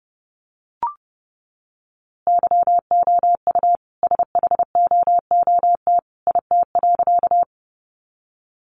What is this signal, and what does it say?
Morse code, heard as a keyed tone, about 30 wpm, 700 Hz. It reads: YOU SHOOT IT.
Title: A Sound Of Thunder 30wpm